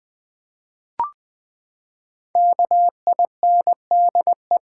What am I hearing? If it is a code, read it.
Morse code, heard as a keyed tone, about 20 wpm, 700 Hz. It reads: KINDE
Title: Build A Word - 5 Letters At A Time - 12-Letter Words - 20wpm